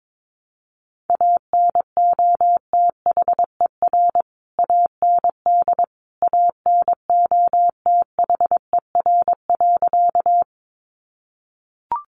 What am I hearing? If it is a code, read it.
Morse code, heard as a keyed tone, about 22 wpm, 700 Hz. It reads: ANOTHER AND ANOTHER.
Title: Sentences from Top 600 Words 22wpm